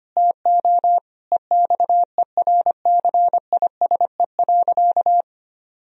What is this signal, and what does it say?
Morse code, heard as a keyed tone, about 25 wpm, 700 Hz. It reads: TO EXERCISE.
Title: Sentences from Top 1000 Words - Review Part 4 - 25wpm